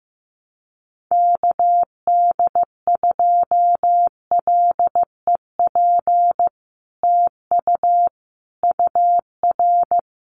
Morse code, heard as a keyed tone, about 15 wpm, 700 Hz. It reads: KD2LEP TU UR
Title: Common QSO Phrases Encore 15wpm